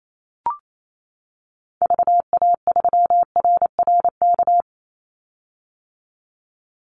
Morse code, heard as a keyed tone, about 28 wpm, 700 Hz. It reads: VA3RRK
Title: Call Signs - 28wpm